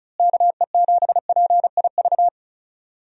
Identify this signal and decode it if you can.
Morse code, heard as a keyed tone, about 36 wpm, 700 Hz. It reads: KE7PIV
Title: USA Call Signs - 6 Characters - Repeated 3x - 35wpm